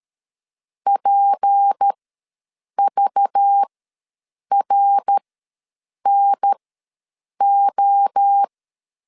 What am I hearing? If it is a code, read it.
Morse code, heard as a keyed tone, about 13 wpm, spaced as for 7 wpm, 800 Hz. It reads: PVRNO
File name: LEZIONI / 6 PKX0 / 6 PK ws 12wpm 3116